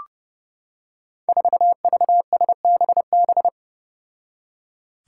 Morse code, heard as a keyed tone, about 28 wpm, 700 Hz. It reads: 4VSBB
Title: Five Challenging Letter-Number Combinations 30wpm